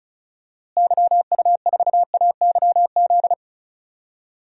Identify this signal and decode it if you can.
Morse code, heard as a keyed tone, about 36 wpm, 700 Hz. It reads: YU4AYZ
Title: DXCC-100 Call Signs - 6 Characters - Repeated 3x - 35wpm